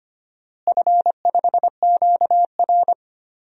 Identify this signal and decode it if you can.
Morse code, heard as a keyed tone, about 25 wpm, 700 Hz. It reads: F5QR
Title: DXCC-100 Call Signs - 4 Characters - Repeated 2x - 25wpm